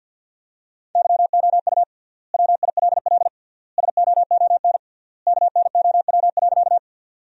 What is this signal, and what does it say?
Morse code, heard as a keyed tone, about 50 wpm, 700 Hz. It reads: YOU WILL SOON KNOW.
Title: 3x Word-Spacing - Sentences from Top 400 Words - 50wpm